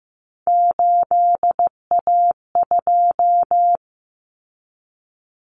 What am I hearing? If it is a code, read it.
Morse code, heard as a keyed tone, about 15 wpm, 700 Hz. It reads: 8A2
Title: Common Cut Numbers - 3-Digit 15wpm